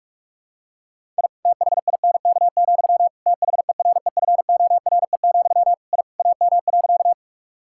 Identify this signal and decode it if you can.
Morse code, heard as a keyed tone, about 42 wpm, 700 Hz. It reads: I THINK, THEREFORE, I AM.
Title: Sentences from Top 1000 Words - Review Part 5 - 45wpm